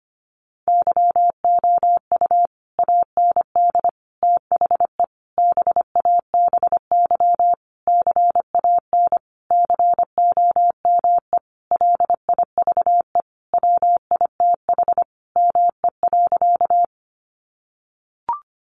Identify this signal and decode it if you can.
Morse code, heard as a keyed tone, about 25 wpm, 700 Hz. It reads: YOU AND THE BABY CAN COME LIVE WITH ME.
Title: Sentences from Top 1300 Words 25wpm